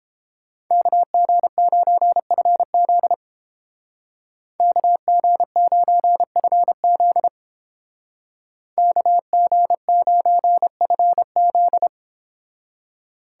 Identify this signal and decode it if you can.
Morse code, heard as a keyed tone, about 30 wpm, 700 Hz. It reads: KG9FZ KG9FZ KG9FZ